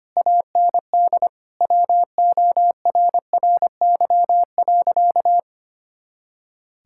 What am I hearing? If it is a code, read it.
Morse code, heard as a keyed tone, about 25 wpm, 700 Hz. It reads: AND WORRY.
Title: Pangram Challenge v2 - 25wpm